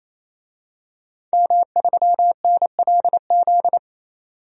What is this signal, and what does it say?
Morse code, heard as a keyed tone, about 27 wpm, 700 Hz. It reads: M3NLZ
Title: DXCC-100 Call Signs - 5 Characters - 28wpm